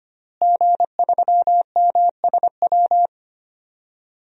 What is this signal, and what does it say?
Morse code, heard as a keyed tone, about 25 wpm, 700 Hz. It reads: G3MSW
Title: DXCC-100 Call Signs - 5 Characters - Repeated 3x - 25wpm